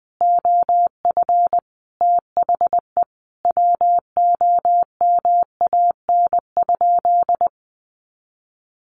Morse code, heard as a keyed tone, about 20 wpm, 700 Hz. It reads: OF THE WOMAN?
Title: Sentences from Top 1400 Words 20wpm